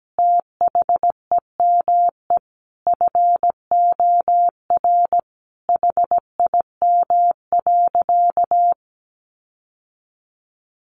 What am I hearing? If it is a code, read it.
Morse code, heard as a keyed tone, about 17 wpm, 700 Hz. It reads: THEME FOR HIM.
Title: Sentences from Top 1100 Words 17wpm